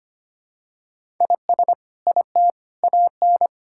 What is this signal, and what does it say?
Morse code, heard as a keyed tone, about 25 wpm, 700 Hz. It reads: IS IT AN
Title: Sentences from Top 1000 Words - Review Part 5 - 25wpm